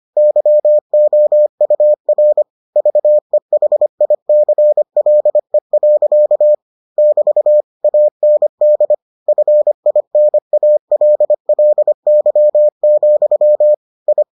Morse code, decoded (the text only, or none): YOUR VEHICLE. <BT> AND FINALLY, I